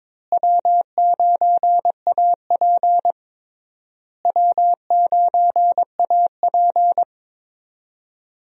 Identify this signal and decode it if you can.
Morse code, heard as a keyed tone, about 23 wpm, 700 Hz. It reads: W9AP W9AP
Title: USA Call Signs - 4 Characters - Repeated 3x - 22wpm